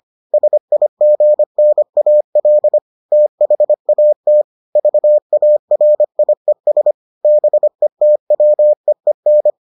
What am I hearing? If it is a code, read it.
Morse code, heard as a keyed tone, about 25 wpm, 600 Hz. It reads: SIGNAL THAT VARIES BETWEEN